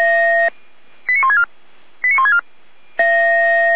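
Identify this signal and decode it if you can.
Five-tone selective call (ZVEI), heard as quick runs of five tones, tones of about 70 ms. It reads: 89164 89164
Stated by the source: Five-tone ZVEI selective call received by SDR.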